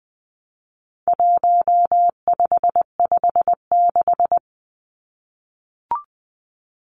Morse code, heard as a keyed tone, about 20 wpm, 700 Hz. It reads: 1556
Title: Numbers - 4-Digit 20wpm